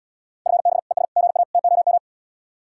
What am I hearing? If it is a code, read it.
Morse code, heard as a keyed tone, about 72 wpm, 700 Hz. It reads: FB ES GA TOM